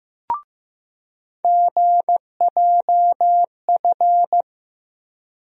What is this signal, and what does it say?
Morse code, heard as a keyed tone, about 15 wpm, 700 Hz. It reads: GJF
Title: USA Call Sign Postfixes - 3 Letters 15wpm